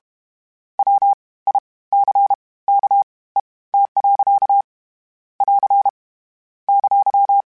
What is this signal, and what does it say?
Morse code, heard as a keyed tone, about 32 wpm, spaced as for 20 wpm, 800 Hz. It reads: WICKET. <AR> !
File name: Misc / Misc-mp3s / WordPairs sp @30